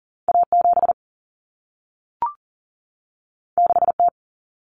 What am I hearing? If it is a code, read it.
Morse code, heard as a keyed tone, about 40 wpm, 700 Hz. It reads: A7 6T
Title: Common Cut Numbers - 2-Digit 40wpm